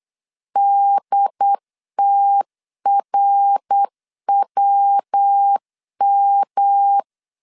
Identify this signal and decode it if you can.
Morse code, heard as a keyed tone, about 8 wpm, 800 Hz. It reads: DTRWM